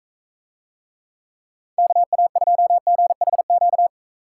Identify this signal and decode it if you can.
Morse code, heard as a keyed tone, about 44 wpm, 700 Hz. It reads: KA2GHQ